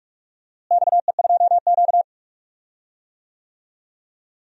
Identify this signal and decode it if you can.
Morse code, heard as a keyed tone, about 46 wpm, 700 Hz. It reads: XE2Q
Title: DXCC-100 Call Signs - 4 Characters - Repeated 3x - 45wpm